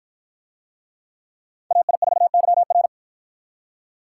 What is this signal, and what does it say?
Morse code, heard as a keyed tone, about 53 wpm, 700 Hz. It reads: AI4YR